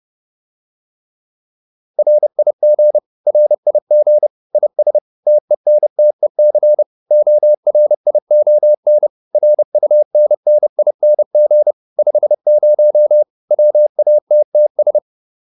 Morse code, heard as a keyed tone, about 30 wpm, 600 Hz. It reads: RIG RIG IS TENTEC ORION RUNNING 50 WATTS
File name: QSO402 30